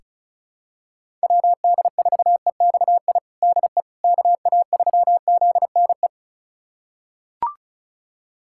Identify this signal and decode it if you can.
Morse code, heard as a keyed tone, about 36 wpm, 690 Hz. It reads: WD4EXI DE KA3ZNE
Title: USA Callsign DE Callsign - 6 Characters - 35wpm